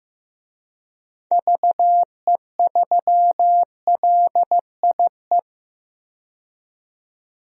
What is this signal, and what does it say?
Morse code, heard as a keyed tone, about 15 wpm, 700 Hz. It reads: VE3LIE
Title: DXCC-100 Call Signs - 6 Characters - 15wpm